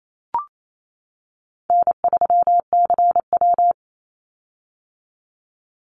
Morse code, heard as a keyed tone, about 28 wpm, 700 Hz. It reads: N3CW
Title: Easy Call Signs - 28wpm